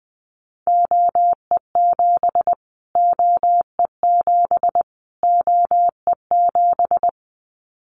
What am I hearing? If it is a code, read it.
Morse code, heard as a keyed tone, about 20 wpm, 700 Hz. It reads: OE7 OE7 OE7
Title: Callsign Prefixes for Top 25 DXCC - Repeated 3x - 20wpm